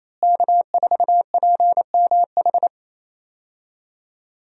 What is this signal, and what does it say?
Morse code, heard as a keyed tone, about 28 wpm, 700 Hz. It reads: K4PMH
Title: USA Call Signs - 5 Characters - Repeated 3x - 28wpm